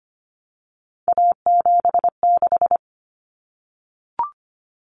Morse code, heard as a keyed tone, about 25 wpm, 700 Hz. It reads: A76
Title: Common Cut Numbers - 3-Digit 25wpm